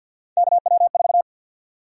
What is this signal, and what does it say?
Morse code, heard as a keyed tone, about 48 wpm, 700 Hz. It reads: KW4
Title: USA Call Sign Prefixes - 50wpm